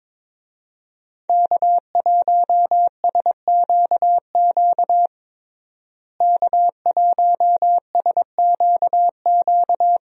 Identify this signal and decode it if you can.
Morse code, heard as a keyed tone, about 22 wpm, 700 Hz. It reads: K1SQQ K1SQQ
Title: USA Call Signs - 5 Characters - Repeated 3x - 22wpm